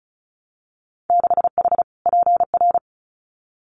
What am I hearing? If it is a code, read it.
Morse code, heard as a keyed tone, about 36 wpm, 700 Hz. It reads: 6H PR
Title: Winter Field Day Exchanges Encore 35wpm